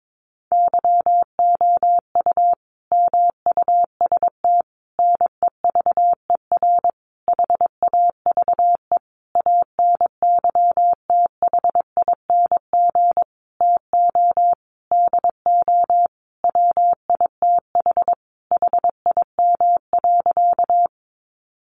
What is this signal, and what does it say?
Morse code, heard as a keyed tone, about 22 wpm, 700 Hz. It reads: YOU MUST NEVER HAVE ANYTHING TO DO WITH HIM.
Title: Sentences from Top 1000 Words 22wpm